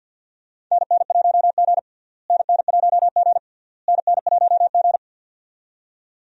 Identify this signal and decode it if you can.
Morse code, heard as a keyed tone, about 51 wpm, 700 Hz. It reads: NN1G NN1G NN1G
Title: USA Call Signs - 4 Characters - Repeated 3x - 50wpm